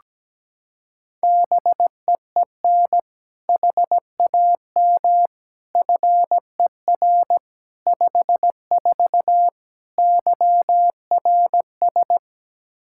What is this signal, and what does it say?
Morse code, heard as a keyed tone, about 17 wpm, 700 Hz. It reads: BEEN HAM FER 54 YRS
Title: Practice QSO - Exchange 3 - More Info - 17wpm